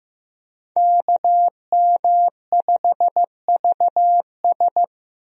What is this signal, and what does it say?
Morse code, heard as a keyed tone, about 15 wpm, 700 Hz. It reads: KM5VS
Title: USA Call Signs - 5 Characters - Repeated 3x - 15wpm